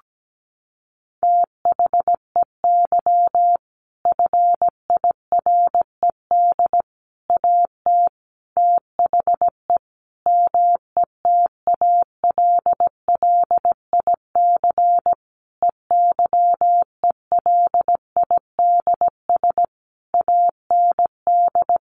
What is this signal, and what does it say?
Morse code, heard as a keyed tone, about 17 wpm, 700 Hz. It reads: THEY FIRED AT THE METALLIC EYELIDS AND
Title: A Sound Of Thunder 17wpm